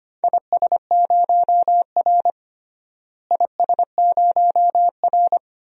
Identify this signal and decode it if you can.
Morse code, heard as a keyed tone, about 25 wpm, 700 Hz. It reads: IS0R IS0R
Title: DXCC-100 Call Signs - 4 Characters - Repeated 2x - 25wpm